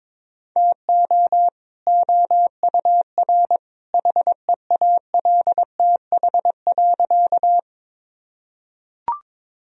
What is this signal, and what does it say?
Morse code, heard as a keyed tone, about 22 wpm, 700 Hz. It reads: TO OUR HEALTH.
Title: Sentences from Top 1000 Words - Review Part 3 - 22wpm